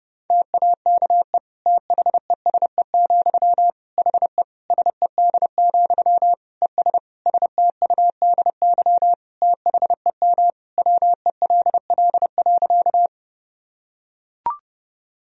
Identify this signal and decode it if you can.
Morse code, heard as a keyed tone, about 32 wpm, 700 Hz. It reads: TAKE THESE, HE SED, ES STUDY THEM WELL.
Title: Sentences from Top 300 Words and Abbreviations 30wpm